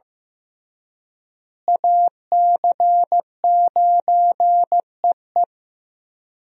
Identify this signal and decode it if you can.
Morse code, heard as a keyed tone, about 15 wpm, 700 Hz. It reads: AC9EE
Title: USA Call Signs - 5 Characters - Repeated 3x - 15wpm